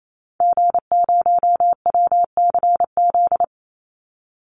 Morse code, heard as a keyed tone, about 28 wpm, 700 Hz. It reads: G0WCZ
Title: Call Signs Repeated 3x - 28wpm